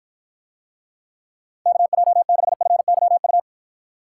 Morse code, heard as a keyed tone, about 53 wpm, 700 Hz. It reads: KJ6FYU